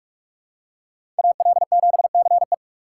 Morse code, heard as a keyed tone, about 46 wpm, 700 Hz. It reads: AL7CE
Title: USA Call Signs - 5 Characters - 3x Character Spacing - 45wpm